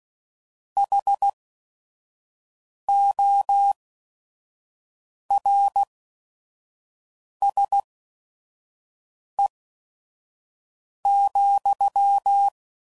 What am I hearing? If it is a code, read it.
Morse code, heard as a keyed tone, about 16 wpm, spaced as for 5 wpm, 800 Hz. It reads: HORSE,